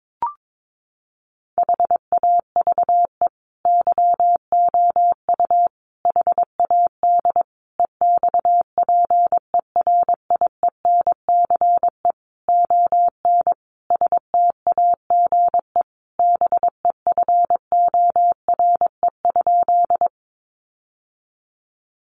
Morse code, heard as a keyed tone, about 22 wpm, 700 Hz. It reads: HAVE YOU HAD EXPERIENCE ON STAGE BEFORE?
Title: Sentences from Top 1200 Words 22wpm